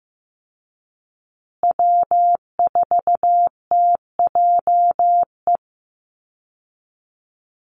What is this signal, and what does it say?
Morse code, heard as a keyed tone, about 15 wpm, 700 Hz. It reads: W4TJE
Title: Call Signs Repeated 2x - 15wpm